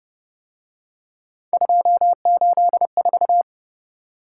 Morse code, heard as a keyed tone, about 31 wpm, 700 Hz. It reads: 284
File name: Speed-Racing - Three Letter-Number Combinations - 25wpm